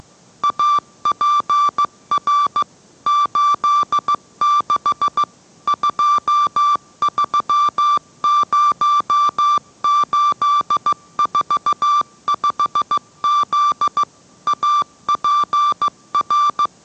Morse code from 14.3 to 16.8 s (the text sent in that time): APR